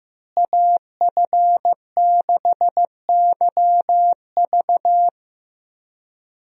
Morse code, heard as a keyed tone, about 15 wpm, 700 Hz. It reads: AF6YV
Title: USA Call Signs - 5 Characters - Repeated 2x - 15wpm